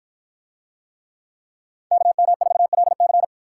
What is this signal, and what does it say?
Morse code, heard as a keyed tone, about 53 wpm, 700 Hz. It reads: KM4LC